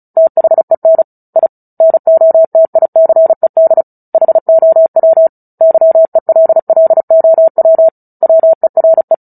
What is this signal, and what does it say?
Morse code, heard as a keyed tone, about 34 wpm, 650 Hz. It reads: THEN I NOTICED HOW YELLOW WERE
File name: Book Two / 09-Chapter 9 - 35 WPM